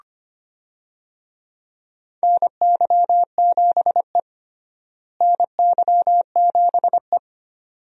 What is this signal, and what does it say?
Morse code, heard as a keyed tone, about 25 wpm, 700 Hz. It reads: NY7E NY7E
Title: USA Call Signs - 4 Characters - Repeated 2x - 25wpm